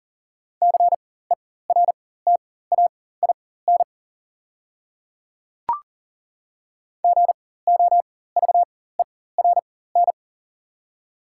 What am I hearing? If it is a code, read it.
Morse code, heard as a keyed tone, about 39 wpm, spaced as for 18 wpm, 700 Hz. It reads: CERTAIN GOVERN
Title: 4x Character Spacing - Top 500 Words - 40wpm